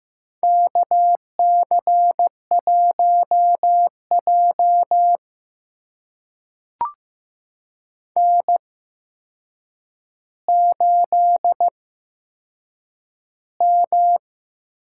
Morse code, heard as a keyed tone, about 15 wpm, spaced as for 7 wpm, 700 Hz. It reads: KC1J N8M
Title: USA Call Signs - 8x Character Spacing - 15wpm